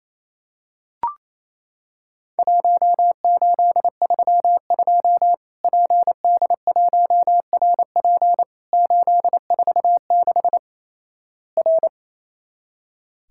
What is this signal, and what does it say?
Morse code, heard as a keyed tone, about 27 wpm, 700 Hz. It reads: 1832 PD1RP 846 R
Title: Worked All Europe DX Contest - QTC Exchange - 28wpm